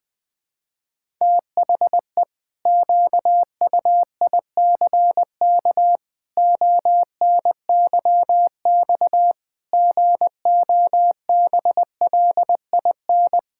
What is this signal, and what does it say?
Morse code, heard as a keyed tone, about 19 wpm, 700 Hz. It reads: THE QUICK ONYX GOBLIN
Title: Pangram Challenge v2 - 20wpm